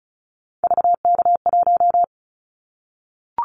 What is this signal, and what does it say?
Morse code, heard as a keyed tone, about 36 wpm, 700 Hz. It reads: VK1
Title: Callsign Prefixes for Top 100 DXCC - 35wpm